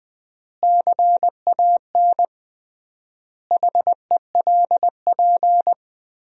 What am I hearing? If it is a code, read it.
Morse code, heard as a keyed tone, about 20 wpm, 700 Hz. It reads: CAN HELP
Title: 3x Word-Spacing - Sentences from Top 200 Words - 20wpm